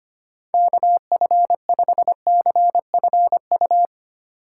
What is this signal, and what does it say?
Morse code, heard as a keyed tone, about 25 wpm, 700 Hz. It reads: KF5CFU
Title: USA Call Signs - 6 Characters - 3x Character Spacing - 25wpm